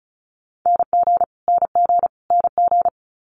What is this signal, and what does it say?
Morse code, heard as a keyed tone, about 36 wpm, 700 Hz. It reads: NG NG NG
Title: Multi-letter Phonemes Repeated 3x 35wpm